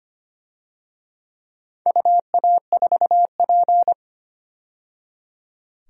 Morse code, heard as a keyed tone, about 25 wpm, 700 Hz. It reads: UA4P
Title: DXCC-100 Call Signs - 4 Characters - 25wpm